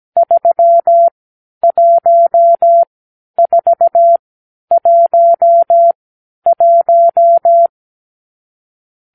Morse code, heard as a keyed tone, about 17 wpm, spaced as for 10 wpm, 680 Hz. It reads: 31411
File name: k7qo / Lesson 46 (1-4)